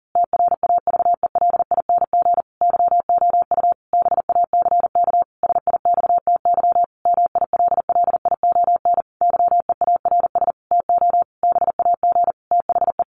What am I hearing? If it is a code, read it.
Morse code, heard as a keyed tone, about 40 wpm, 700 Hz. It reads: TRAVELING YOU BACK SIXTY MILLION YEARS TO BAG THE
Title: A Sound Of Thunder 40wpm